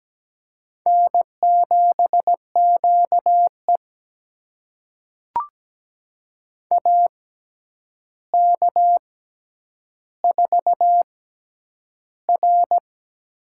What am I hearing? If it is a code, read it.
Morse code, heard as a keyed tone, about 17 wpm, spaced as for 9 wpm, 700 Hz. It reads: N7QE AK4R
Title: USA Call Signs - 4 Characters - 6x Character Spacing - 17wpm